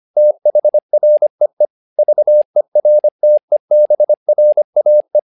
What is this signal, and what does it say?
Morse code, heard as a keyed tone, about 25 wpm, 600 Hz. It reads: THREE VERTEBRAE
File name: learning center / tower trouble / tower trouble 25wpm